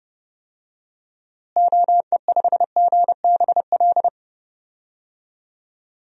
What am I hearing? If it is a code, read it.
Morse code, heard as a keyed tone, about 30 wpm, 700 Hz. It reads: OE5GBL
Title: DXCC-100 Call Signs - 6 Characters - Repeated 2x - 30wpm